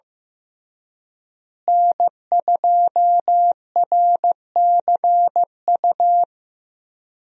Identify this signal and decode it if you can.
Morse code, heard as a keyed tone, about 15 wpm, 700 Hz. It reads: N2RCU
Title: USA Call Signs - 5 Characters - Repeated 2x - 15wpm